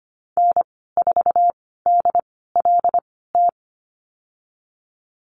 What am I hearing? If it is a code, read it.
Morse code, heard as a keyed tone, about 25 wpm, spaced as for 16 wpm, 700 Hz. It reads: N4DLT
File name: Lesson 15g - Number 5 - Call Signs 25wpm